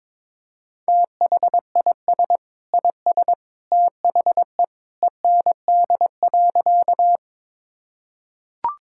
Morse code, heard as a keyed tone, about 22 wpm, 700 Hz. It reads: THIS IS THE END.
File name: practice / 22wpm / Sentences with 2-Word Phrases - 22wpm